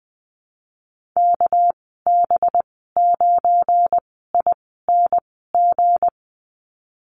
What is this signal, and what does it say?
Morse code, heard as a keyed tone, about 20 wpm, 700 Hz. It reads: K B 9 I N G
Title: 238 Lesson 38g - Letter q - Call Signs 20wpm